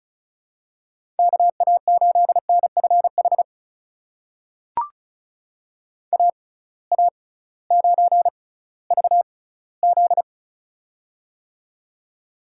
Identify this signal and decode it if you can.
Morse code, heard as a keyed tone, about 36 wpm, spaced as for 19 wpm, 700 Hz. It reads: KA8NFH AA9VZ
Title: USA Call Signs - 6x Character Spacing - 35wpm